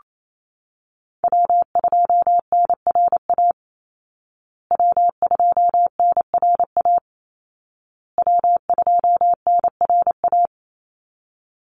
Morse code, heard as a keyed tone, about 28 wpm, 700 Hz. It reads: W2NRA W2NRA W2NRA
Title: Call Signs Repeated 2x - 28wpm